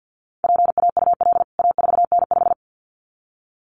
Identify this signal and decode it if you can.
Morse code, heard as a keyed tone, about 48 wpm, 700 Hz. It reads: PAUL A4N5
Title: CWT Exchange 50wpm